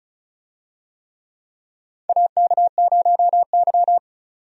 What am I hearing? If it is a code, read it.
Morse code, heard as a keyed tone, about 36 wpm, 700 Hz. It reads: AK0Y
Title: USA Call Signs - 4 Characters - 35wpm